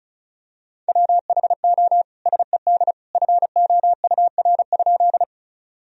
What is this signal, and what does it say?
Morse code, heard as a keyed tone, about 36 wpm, 700 Hz. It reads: WHO SED FOUR?
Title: Sentences from Top 300 Words and Abbreviations 35wpm